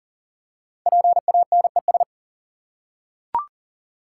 Morse code, heard as a keyed tone, about 39 wpm, 700 Hz. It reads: PANES
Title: Code Groups - 5 Letters - 40wpm